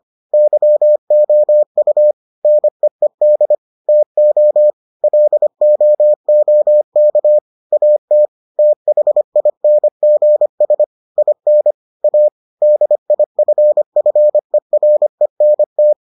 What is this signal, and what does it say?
Morse code, heard as a keyed tone, about 25 wpm, 600 Hz. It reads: YOU NEED TO LOOK AT THINGS IN A DIFFERENT